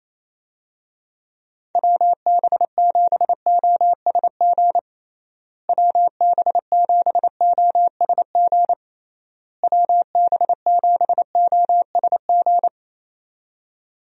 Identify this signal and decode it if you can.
Morse code, heard as a keyed tone, about 28 wpm, 700 Hz. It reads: WB7OSG WB7OSG WB7OSG
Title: USA Call Signs - 6 Characters - Repeated 3x - 28wpm